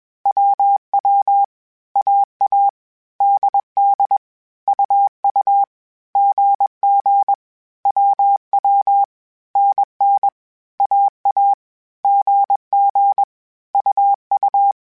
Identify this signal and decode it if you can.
Morse code, heard as a keyed tone, about 21 wpm, 800 Hz. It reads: WW AA DD UU GG WW NN AA GG UU
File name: Groups@20 / Groups@20 mp3s / AWUNDG@20